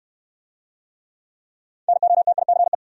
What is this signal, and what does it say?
Morse code, heard as a keyed tone, about 64 wpm, 700 Hz. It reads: NOTICE